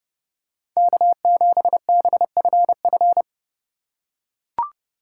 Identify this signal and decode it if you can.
Morse code, heard as a keyed tone, about 30 wpm, 700 Hz. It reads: K7BFF
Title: USA Call Signs - 5 Characters - 3x Character Spacing - 30wpm